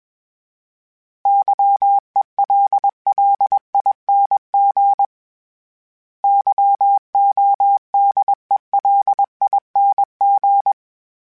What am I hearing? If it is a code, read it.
Morse code, heard as a keyed tone, about 21 wpm, 800 Hz. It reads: YELLING YODELING
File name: Special INGs / Special INGs-mp3s / Sngl Ing @20